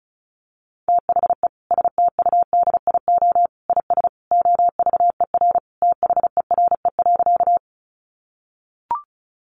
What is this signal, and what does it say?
Morse code, heard as a keyed tone, about 34 wpm, 700 Hz. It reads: THE STUDIO IS OVER THERE.
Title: Sentences from Top 1200 Words 35wpm